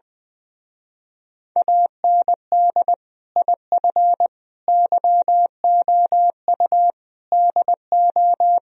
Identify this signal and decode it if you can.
Morse code, heard as a keyed tone, about 21 wpm, 700 Hz. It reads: AND IF YOU DO
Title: Speed-Racing - Sets of 5 Words 20wpm